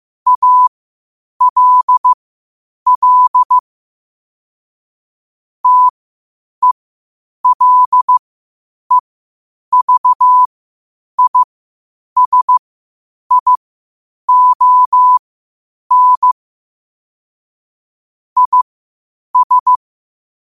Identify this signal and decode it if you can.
Morse code, heard as a keyed tone, about 15 wpm, spaced as for 8 wpm, 1000 Hz. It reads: ALL TELEVISION IS
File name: 8wpm